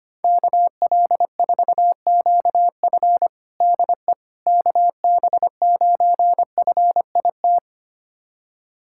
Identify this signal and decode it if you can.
Morse code, heard as a keyed tone, about 25 wpm, 700 Hz. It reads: KL4QF DE KB9FIT
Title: USA Callsign DE Callsign 25wpm